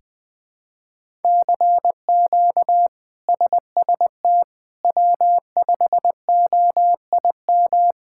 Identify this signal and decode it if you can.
Morse code, heard as a keyed tone, about 20 wpm, 700 Hz. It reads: CQ SST W5OIM
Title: K1USN SST QSO Practice - 20wpm